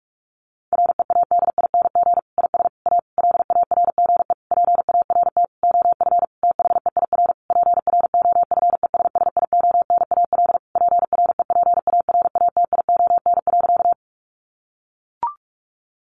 Morse code, heard as a keyed tone, about 44 wpm, 700 Hz. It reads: READING IS A LARGE PART OF THEIR PROFESSIONAL PREPARATION.
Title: Sentences from Top 1400 Words 45wpm